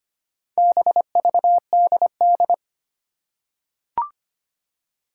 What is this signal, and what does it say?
Morse code, heard as a keyed tone, about 25 wpm, 700 Hz. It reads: BVDD
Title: Four Challenging Letter-Number Combinations 25wpm